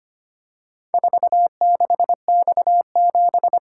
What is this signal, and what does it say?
Morse code, heard as a keyed tone, about 25 wpm, 700 Hz. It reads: 46X7
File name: practice / 25wpm / Four Letter-Number Combinations - 25wpm